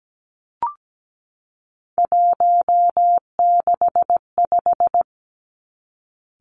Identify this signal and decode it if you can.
Morse code, heard as a keyed tone, about 17 wpm, 700 Hz. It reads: 165
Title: Three Challenging Letter-Number Combinations 17wpm